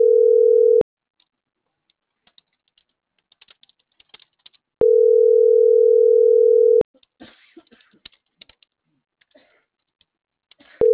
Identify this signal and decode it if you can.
Ringback tone (North America), 440 + 480 Hz.